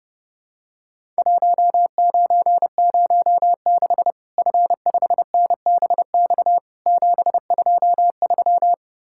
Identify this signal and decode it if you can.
Morse code, heard as a keyed tone, about 30 wpm, 700 Hz. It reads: 1906 F5NBX 723
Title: Worked All Europe DX Contest - QTC Exchange - 30wpm